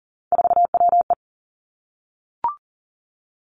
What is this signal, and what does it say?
Morse code, heard as a keyed tone, about 40 wpm, 700 Hz. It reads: 4WE
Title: Three Challenging Letter-Number Combinations 40wpm